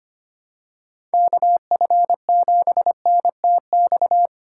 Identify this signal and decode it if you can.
Morse code, heard as a keyed tone, about 25 wpm, 700 Hz. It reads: KF7NTX
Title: USA Call Signs - 8x Character Spacing - 25wpm